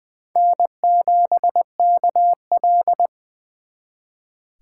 Morse code, heard as a keyed tone, about 24 wpm, 700 Hz. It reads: N7KL
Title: Speed Racing - USA 1x2 Call Signs 20wpm